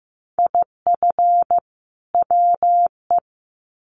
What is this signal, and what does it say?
Morse code, heard as a keyed tone, about 15 wpm, 700 Hz. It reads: IF WE
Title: Sets of 2 Words 15wpm